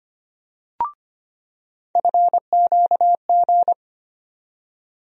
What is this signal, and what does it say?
Morse code, heard as a keyed tone, about 25 wpm, 700 Hz. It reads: FQG
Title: USA Call Sign Postfixes - 3 Letters 25wpm